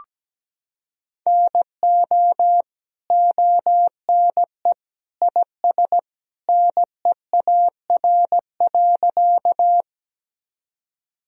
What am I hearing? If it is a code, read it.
Morse code, heard as a keyed tone, about 17 wpm, 700 Hz. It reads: NO ONE IS NEAR.
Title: Sentences from Top 1000 Words - Review Part 1 - 17wpm